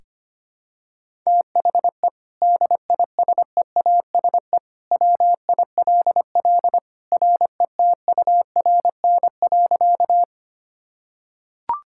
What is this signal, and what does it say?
Morse code, heard as a keyed tone, about 25 wpm, 700 Hz. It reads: THE DISEASE WILL RETURN.
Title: Sentences from Top 1000 Words - Review Part 6 - 25wpm